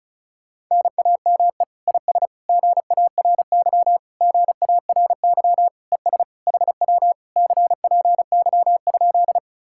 Morse code, heard as a keyed tone, about 40 wpm, 700 Hz. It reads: NAME IS GARY GARY ES HW CPY?